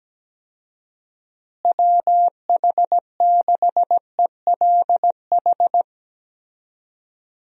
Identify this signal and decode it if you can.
Morse code, heard as a keyed tone, about 17 wpm, 700 Hz. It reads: WH6ELH
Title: USA Call Signs - 6 Characters - Repeated 2x - 17wpm